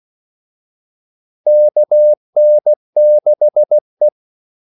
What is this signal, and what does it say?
Morse code, heard as a keyed tone, about 16 wpm, 600 Hz. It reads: KN6E
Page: Kaldesignaler 16 wpm